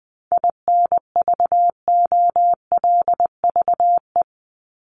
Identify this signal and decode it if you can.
Morse code, heard as a keyed tone, about 20 wpm, 700 Hz. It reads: INVOLVE
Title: ING Suffix - Top 100 Words with Base Words 20wpm